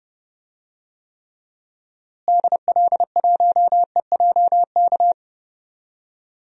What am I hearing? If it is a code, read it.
Morse code, heard as a keyed tone, about 31 wpm, 700 Hz. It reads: DL1EJK
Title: DXCC-100 Call Signs - 6 Characters - Repeated 3x - 30wpm